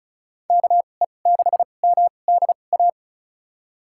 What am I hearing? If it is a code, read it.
Morse code, heard as a keyed tone, about 34 wpm, spaced as for 25 wpm, 700 Hz. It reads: KE6MDA
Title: USA Call Signs - 2x Character Spacing - 35wpm